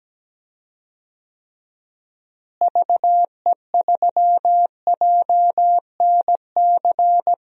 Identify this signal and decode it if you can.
Morse code, heard as a keyed tone, about 17 wpm, 700 Hz. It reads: VE3JNC
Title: DXCC-100 Call Signs - 6 Characters - Repeated 2x - 17wpm